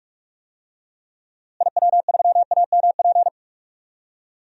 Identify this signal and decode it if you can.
Morse code, heard as a keyed tone, about 44 wpm, 700 Hz. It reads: IW3AMP
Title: DXCC-100 Call Signs - 6 Characters - Repeated 3x - 45wpm